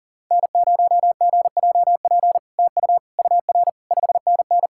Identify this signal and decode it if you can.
Morse code, heard as a keyed tone, about 37 wpm, 690 Hz. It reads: N0GJP TU UR 5NN
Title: SOTA QSO Practice 40wpm